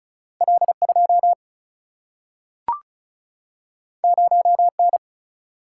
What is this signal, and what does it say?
Morse code, heard as a keyed tone, about 36 wpm, 700 Hz. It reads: L2 0N
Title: Single-Letter Single-Number 35wpm